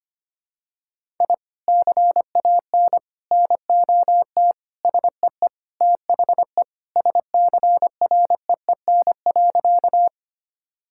Morse code, heard as a keyed tone, about 25 wpm, 700 Hz. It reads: I CAN NOT SEE THE SCREEN.
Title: Sentences from Top 1000 Words - Review Part 6 - 25wpm